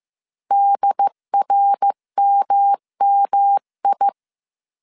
Morse code, heard as a keyed tone, about 15 wpm, 800 Hz. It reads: DRMMI
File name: CW / DET / 1 3 1500C S C 14wpm 2458